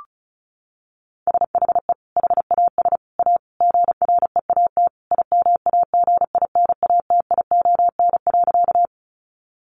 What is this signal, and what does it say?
Morse code, heard as a keyed tone, about 34 wpm, 700 Hz. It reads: SHE HAS A GREAT IMAGINATION.
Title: Sentences from Top 1300 Words 35wpm